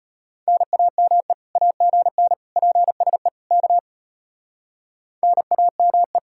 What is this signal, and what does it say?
Morse code, heard as a keyed tone, about 33 wpm, 700 Hz. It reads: NAME AGN PSE K NAME